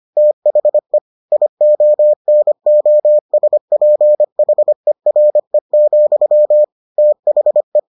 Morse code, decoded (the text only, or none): THE IONOSPHERE, THE